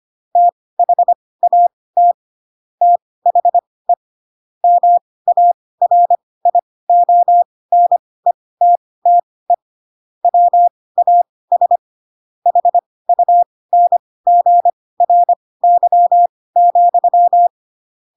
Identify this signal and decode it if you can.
Morse code, heard as a keyed tone, about 25 wpm, spaced as for 18 wpm, 700 Hz. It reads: THAT THE MARIONETTE WAS HUNGRY,
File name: Pinocchio / Pinocchio ch7 12 25 ik1hge